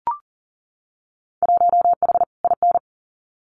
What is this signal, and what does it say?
Morse code, heard as a keyed tone, about 41 wpm, 700 Hz. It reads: 1H IN
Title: Winter Field Day Exchanges 40wpm